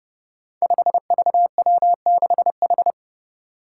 Morse code, heard as a keyed tone, about 32 wpm, 700 Hz. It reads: 5VW6H
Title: Five Challenging Letter-Number Combinations 30wpm